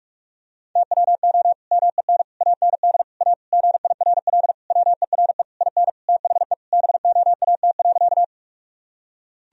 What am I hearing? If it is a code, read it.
Morse code, heard as a keyed tone, about 46 wpm, 700 Hz. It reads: TWO MEN AND A GIRL WERE IN THE BOAT.
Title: Sentences from Top 1000 Words - Review Part 3 - 45wpm